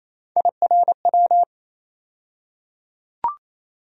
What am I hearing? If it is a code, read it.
Morse code, heard as a keyed tone, about 28 wpm, 700 Hz. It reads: IRW
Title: USA Call Sign Postfixes - 3 Letters 28wpm